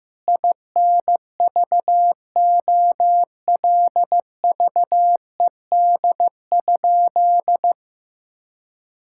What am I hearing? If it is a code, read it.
Morse code, heard as a keyed tone, about 15 wpm, 700 Hz. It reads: INVOLVED?
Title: Sentences from Top 1000 Words - Review Part 5 - 15wpm